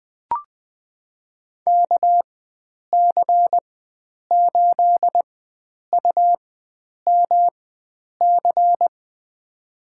Morse code, heard as a keyed tone, about 20 wpm, spaced as for 9 wpm, 700 Hz. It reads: KC8UMC
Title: USA Call Signs - 6 Characters - 4x Character Spacing - 20wpm